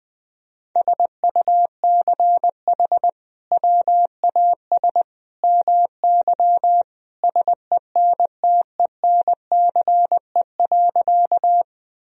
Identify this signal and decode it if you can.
Morse code, heard as a keyed tone, about 20 wpm, 700 Hz. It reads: SUCH WAS MY SENTENCE.
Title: Sentences from Top 1000 Words - Review Part 1 - 20wpm